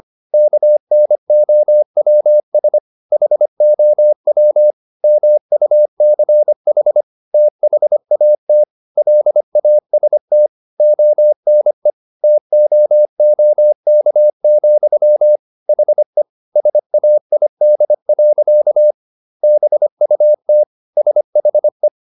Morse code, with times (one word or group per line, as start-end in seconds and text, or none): 0.3-2.8 KNOWS
3.1-4.7 HOW
5.0-7.0 MUCH
7.3-8.6 THAT
9.0-10.5 LAST
10.8-11.9 ONE
12.2-15.4 TOOK,
15.7-16.2 HE
16.6-18.9 SAID.
19.4-20.6 BUT
21.0-21.9 SHE